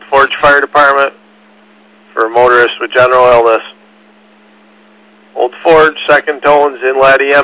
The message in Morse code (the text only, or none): none